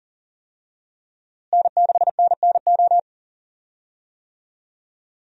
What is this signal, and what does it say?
Morse code, heard as a keyed tone, about 40 wpm, 700 Hz. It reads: N6NNO
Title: USA Call Signs - 5 Characters - Repeated 3x - 40wpm